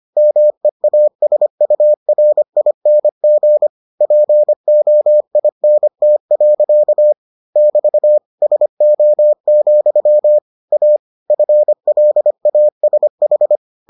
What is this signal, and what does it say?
Morse code, heard as a keyed tone, about 25 wpm, 600 Hz. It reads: MEASURING POINT. <BT> SO, A FLASH